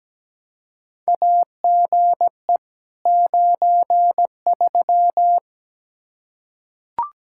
Morse code, heard as a keyed tone, about 17 wpm, 700 Hz. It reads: AGE 93
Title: K1USN SST QSO Practice 17wpm